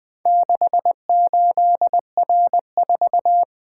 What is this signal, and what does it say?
Morse code, heard as a keyed tone, about 24 wpm, 700 Hz. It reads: 68R4
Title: Speed-Racing - Four Letter-Number Combinations - 20wpm